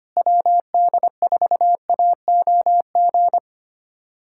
Morse code, heard as a keyed tone, about 25 wpm, 700 Hz. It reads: WD4AOG
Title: USA Callsign DE Callsign 25wpm